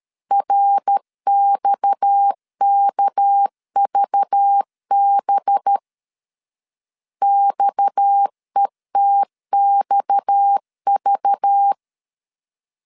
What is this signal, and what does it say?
Morse code, heard as a keyed tone, about 13 wpm, 800 Hz. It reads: RXKVB XETXV
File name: LEZIONI / 6 PKX0 / 6 PKX 12wpm 2024